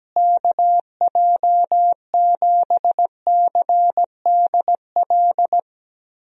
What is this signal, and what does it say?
Morse code, heard as a keyed tone, about 17 wpm, 700 Hz. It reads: KJ7CDL
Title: USA Call Signs - 6 Characters - Repeated 3x - 17wpm